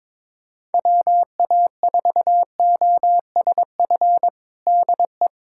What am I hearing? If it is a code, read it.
Morse code, heard as a keyed tone, about 22 wpm, 700 Hz. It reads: WA4OSF DE
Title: USA Callsign DE Callsign - 6 Characters - 22wpm